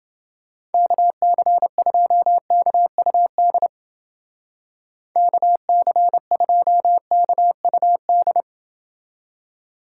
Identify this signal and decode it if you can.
Morse code, heard as a keyed tone, about 30 wpm, 700 Hz. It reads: KC2KUD KC2KUD